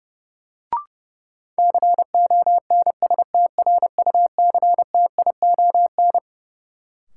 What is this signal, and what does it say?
Morse code, heard as a keyed tone, about 30 wpm, 700 Hz. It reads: CONSTRUCTION
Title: Sentences from Top 1600 Words - New Word Drill - 30wpm